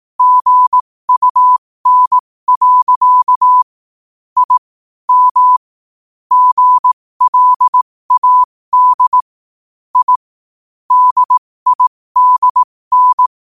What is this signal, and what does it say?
Morse code, heard as a keyed tone, about 18 wpm, spaced as for 15 wpm, 1000 Hz. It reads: GUN. I M GLAD I DIDN
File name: qotdfiles / 15wpm